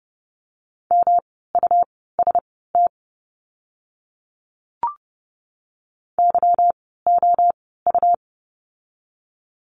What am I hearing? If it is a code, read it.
Morse code, heard as a keyed tone, about 30 wpm, spaced as for 17 wpm, 700 Hz. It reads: MUST YOU
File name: course / 30:10wpm / 237 Lesson 38f - Letter q - Words 30wpm